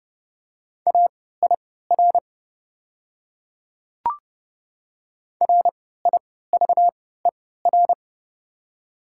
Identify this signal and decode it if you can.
Morse code, heard as a keyed tone, about 30 wpm, spaced as for 17 wpm, 700 Hz. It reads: AIR RIVER
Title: 3x Character Spacing - Top 300 Words - 30wpm